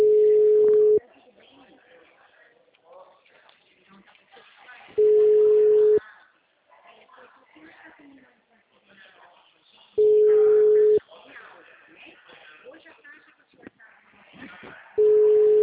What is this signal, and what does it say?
Ringback tone (Europe, 425 Hz), with speech.
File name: out- -2010- - - .12296